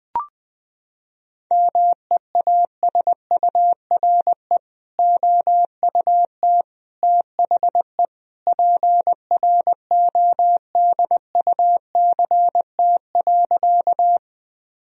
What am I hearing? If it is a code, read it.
Morse code, heard as a keyed tone, about 20 wpm, 700 Hz. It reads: MEASURE OUT THE PRODUCT.
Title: Sentences from Top 1000 Words - Review Part 2 - 20wpm